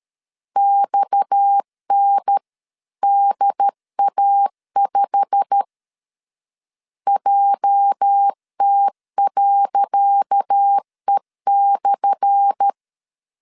Morse code, heard as a keyed tone, about 12 wpm, 800 Hz. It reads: XNDA5 JT.E/
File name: LEZIONI / 10 SIGNS / 10C DOT 12wpm 2451